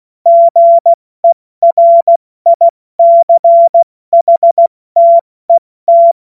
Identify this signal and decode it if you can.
Morse code, heard as a keyed tone, about 16 wpm, spaced as for 14 wpm, 680 Hz. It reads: GERICHTET
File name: Morsekurs / Witze / Witz 20. Juni 2023 70 F80